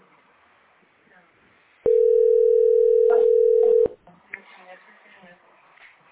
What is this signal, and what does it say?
Ringback tone (North America), 440 + 480 Hz, with speech.
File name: c1d170e1-4658-48db-bc42